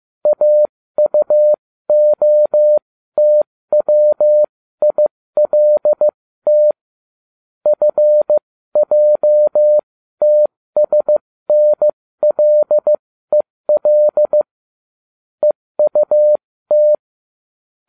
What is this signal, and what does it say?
Morse code, heard as a keyed tone, about 15 wpm, spaced as for 13 wpm, 600 Hz. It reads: AUOTWILT FJTSNLEL EUT